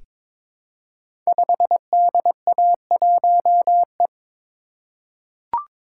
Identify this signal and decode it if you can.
Morse code, heard as a keyed tone, about 22 wpm, 700 Hz. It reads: 5DA1E
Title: Code Groups - 5 Letter:Numbers - 22wpm